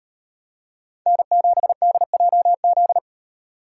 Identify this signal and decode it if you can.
Morse code, heard as a keyed tone, about 35 wpm, 700 Hz. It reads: N7DJZ